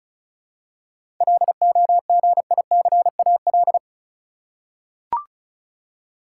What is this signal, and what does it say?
Morse code, heard as a keyed tone, about 34 wpm, 700 Hz. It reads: LOGICAL
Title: Seven Letter Words - 35wpm